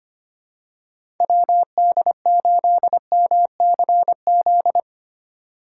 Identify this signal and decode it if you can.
Morse code, heard as a keyed tone, about 25 wpm, 700 Hz. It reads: WD8MCZ
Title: USA Call Signs - 6 Characters - 2x Character Spacing - 25wpm